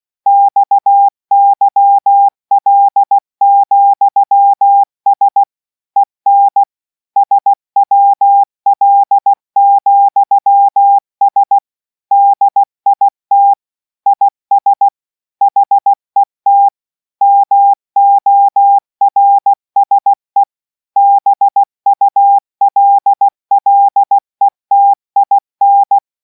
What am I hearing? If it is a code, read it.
Morse code, heard as a keyed tone, about 16 wpm, 800 Hz. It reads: XYL,S EN SWL,S DIT IS HET MORSE BULLETIN